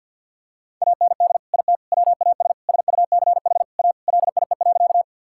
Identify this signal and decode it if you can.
Morse code, heard as a keyed tone, about 50 wpm, 700 Hz. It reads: AND IT WAS SUCH A LIE.
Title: Sentences from Top 1600 Words 50wpm